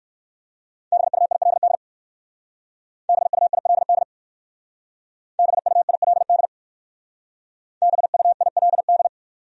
Morse code, heard as a keyed tone, about 59 wpm, 700 Hz. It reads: BUILD BUILD BUILD BUILD